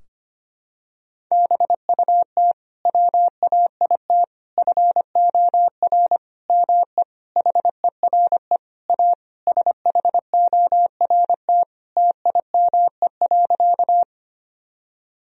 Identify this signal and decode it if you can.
Morse code, heard as a keyed tone, about 25 wpm, 700 Hz. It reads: BUT WAIT FOR ME HERE A SHORT TIME.
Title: Sentences from Top 1000 Words - Review Part 2 - 25wpm